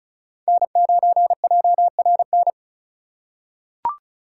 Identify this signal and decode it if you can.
Morse code, heard as a keyed tone, about 34 wpm, 700 Hz. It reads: N9JRN
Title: USA Call Signs - 5 Characters - 4x Character Spacing - 35wpm